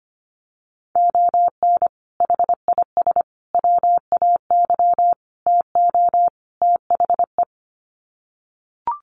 Morse code, heard as a keyed tone, about 25 wpm, 700 Hz. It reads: ON HIS WAY TO THE
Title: Sets of 5 Words Encore2 25wpm